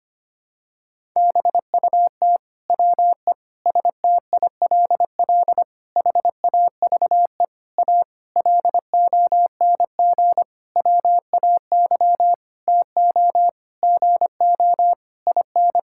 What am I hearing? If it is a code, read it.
Morse code, heard as a keyed tone, about 25 wpm, 700 Hz. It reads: BUT WE STILL HAVE A LONG WAY TO GO IN
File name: practice / 25wpm / Sentences with 9-Word Phrases - 25wpm